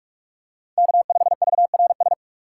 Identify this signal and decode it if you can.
Morse code, heard as a keyed tone, about 46 wpm, 700 Hz. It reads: K5VRS
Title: USA Call Signs - 5 Characters - 4x Character Spacing - 45wpm